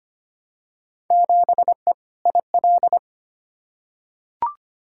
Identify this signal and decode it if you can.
Morse code, heard as a keyed tone, about 25 wpm, 700 Hz. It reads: 7E IL
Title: Field Day Exchanges Encore - 25wpm